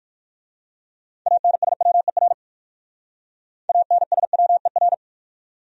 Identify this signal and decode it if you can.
Morse code, heard as a keyed tone, about 51 wpm, 690 Hz. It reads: ANSWER ANSWER
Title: Speed-Racing Top 500 Words 45wpm